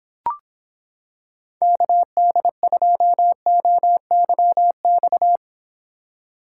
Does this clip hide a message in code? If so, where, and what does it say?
Yes: Morse code from 1.6 to 5.4 s, heard as a keyed tone, about 26 wpm, 700 Hz. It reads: KD2OYX